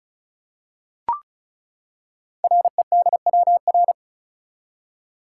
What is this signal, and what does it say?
Morse code, heard as a keyed tone, about 36 wpm, 700 Hz. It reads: REDWR
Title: Code Groups - 5 Letters - 35wpm